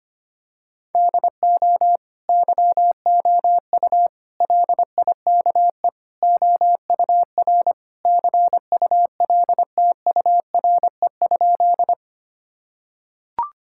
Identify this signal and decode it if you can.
Morse code, heard as a keyed tone, about 25 wpm, 700 Hz. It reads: DO YOU LIKE OUR CULTURE?
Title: Sentences from Top 1000 Words - Review Part 6 - 25wpm